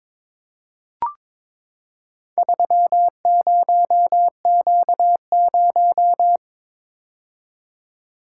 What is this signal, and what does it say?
Morse code, heard as a keyed tone, about 22 wpm, 700 Hz. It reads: 30Q0
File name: practice / 22wpm / Four Letter-Number Combinations - 22wpm